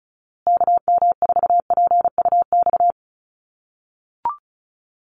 Morse code, heard as a keyed tone, about 34 wpm, 700 Hz. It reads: KM4PUX
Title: USA Call Signs Repeated 3x 35wpm